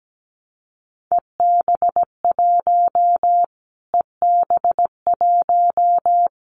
Morse code, heard as a keyed tone, about 17 wpm, 700 Hz. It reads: EB1 EB1
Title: Callsign Prefixes for Top 25 DXCC - Repeated 3x - 17wpm